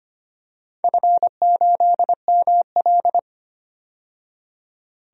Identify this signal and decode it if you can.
Morse code, heard as a keyed tone, about 25 wpm, 700 Hz. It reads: F8ML
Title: DXCC-100 Call Signs - 4 Characters - 25wpm